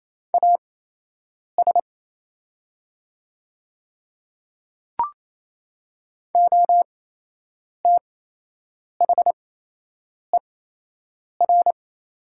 Morse code, heard as a keyed tone, about 28 wpm, spaced as for 8 wpm, 700 Hz. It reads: AS OTHER
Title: 8x Character Spacing - Top 100 Words - 28wpm